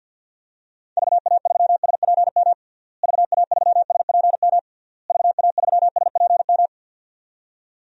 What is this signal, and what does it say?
Morse code, heard as a keyed tone, about 48 wpm, 700 Hz. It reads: VA3SPM VA3SPM VA3SPM
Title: DXCC-100 Call Signs - 6 Characters - Repeated 3x - 50wpm